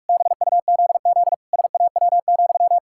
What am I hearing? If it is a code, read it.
Morse code, heard as a keyed tone, about 46 wpm, 700 Hz. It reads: BUZZ SAW,
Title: Pangram Challenge v2 - 45wpm